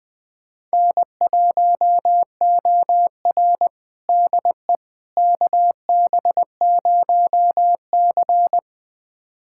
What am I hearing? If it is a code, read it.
Morse code, heard as a keyed tone, about 20 wpm, 700 Hz. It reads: N1OR DE KB0C
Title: USA Callsign DE Callsign - 4 Characters - 20wpm